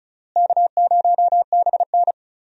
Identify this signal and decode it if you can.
Morse code, heard as a keyed tone, about 34 wpm, 700 Hz. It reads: K0BN
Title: USA Call Signs - 4 Characters - Repeated 3x - 35wpm